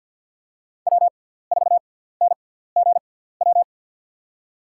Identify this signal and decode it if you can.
Morse code, heard as a keyed tone, about 52 wpm, spaced as for 17 wpm, 690 Hz. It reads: W4NGW
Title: USA Call Signs - 5 Characters - 6x Character Spacing - 50wpm